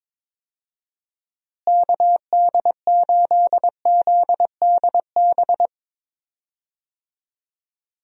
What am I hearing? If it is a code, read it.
Morse code, heard as a keyed tone, about 22 wpm, 700 Hz. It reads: KD8ZDB
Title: USA Call Signs - 6 Characters - Repeated 3x - 22wpm